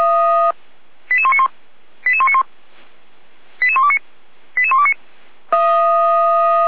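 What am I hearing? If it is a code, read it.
Five-tone selective call (ZVEI), heard as quick runs of five tones, tones of about 70 ms. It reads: 88191 88191 88129 88129
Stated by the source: ZVEI five-tone call received by SDR, from feuerwehr-orlamuende.org fire brigade audio server.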